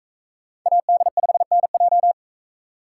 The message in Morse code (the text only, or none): AD5NJ